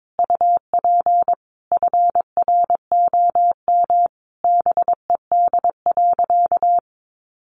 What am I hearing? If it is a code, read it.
Morse code, heard as a keyed tone, about 22 wpm, 700 Hz. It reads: UP FROM BED.
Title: Sentences from Top 900 Words 22wpm